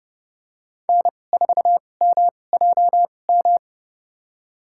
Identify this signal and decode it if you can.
Morse code, heard as a keyed tone, about 30 wpm, spaced as for 22 wpm, 690 Hz. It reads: N4MJM
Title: USA Call Signs - 5 Characters - 2x Character Spacing - 30wpm